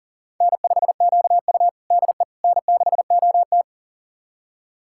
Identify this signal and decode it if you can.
Morse code, heard as a keyed tone, about 43 wpm, 690 Hz. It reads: N5QU DE N6OT
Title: USA Callsign DE Callsign - 4 Characters - 40wpm